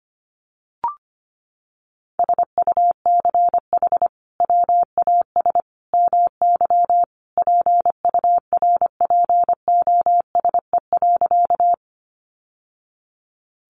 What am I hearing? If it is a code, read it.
Morse code, heard as a keyed tone, about 25 wpm, 700 Hz. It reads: SUCH WAS MY PURPOSE.
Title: Sentences from Top 800 Words 25wpm